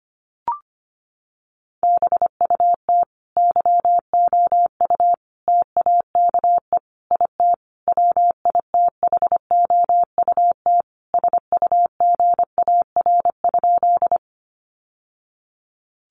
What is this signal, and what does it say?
Morse code, heard as a keyed tone, about 25 wpm, 700 Hz. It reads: BUT YOU TAKE IT WITHOUT SUGAR?
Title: Sentences from Top 1200 Words 25wpm